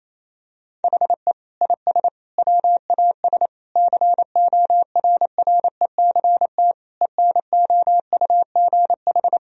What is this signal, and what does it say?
Morse code, heard as a keyed tone, about 27 wpm, 700 Hz. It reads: HE IS WAS CORRECT ENOUGH
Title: Sentences from Top 1000 Words - Review Part 2 - 28wpm